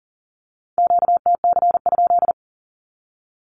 Morse code, heard as a keyed tone, about 40 wpm, 700 Hz. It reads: QTC?
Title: Q-Codes with Spoken Meaning - 40wpm